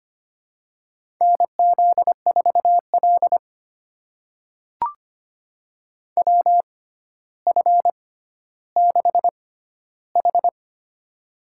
Morse code, heard as a keyed tone, about 25 wpm, spaced as for 13 wpm, 700 Hz. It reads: NZ4L WF6H
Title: USA Call Signs - 6x Character Spacing - 25wpm